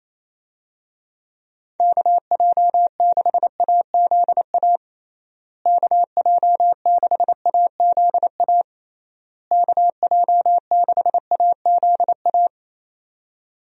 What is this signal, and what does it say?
Morse code, heard as a keyed tone, about 28 wpm, 700 Hz. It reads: KJ6AZA KJ6AZA KJ6AZA
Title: USA Call Signs - 6 Characters - Repeated 3x - 28wpm